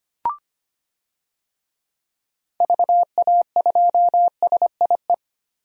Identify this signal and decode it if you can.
Morse code, heard as a keyed tone, about 25 wpm, 700 Hz. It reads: VA2SIE
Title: DXCC-100 Call Signs - 6 Characters - Repeated 3x - 25wpm